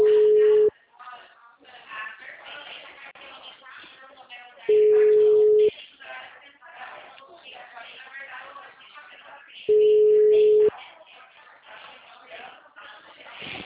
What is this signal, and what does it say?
Ringback tone (Europe, 425 Hz), with speech.